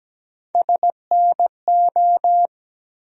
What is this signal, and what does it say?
Morse code, heard as a keyed tone, about 17 wpm, 700 Hz. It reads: SNO
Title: USA Call Sign Postfixes - 3 Letters 17wpm